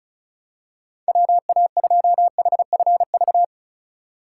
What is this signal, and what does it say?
Morse code, heard as a keyed tone, about 36 wpm, 700 Hz. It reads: WA2HFV
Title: USA Call Signs - 6 Characters - 2x Character Spacing - 35wpm